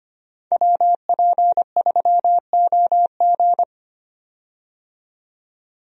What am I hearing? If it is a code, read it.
Morse code, heard as a keyed tone, about 25 wpm, 700 Hz. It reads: WP3OG
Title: USA Call Signs - 5 Characters - 25wpm